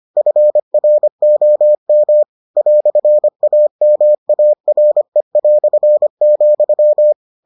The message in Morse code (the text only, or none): FROM "AMARE",